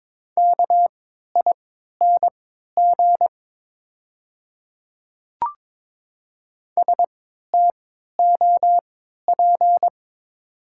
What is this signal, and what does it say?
Morse code, heard as a keyed tone, about 22 wpm, spaced as for 13 wpm, 690 Hz. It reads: KING STOP
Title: 3x Character Spacing - Top 500 Words - 22wpm